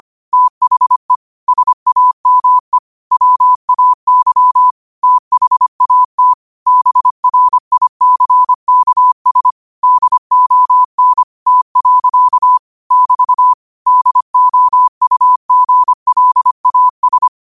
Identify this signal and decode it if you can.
Morse code, heard as a keyed tone, about 25 wpm, 1000 Hz. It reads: THE SAME WAY THAT BRICKS DON T. <BT> DOUGLAS
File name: 25wpm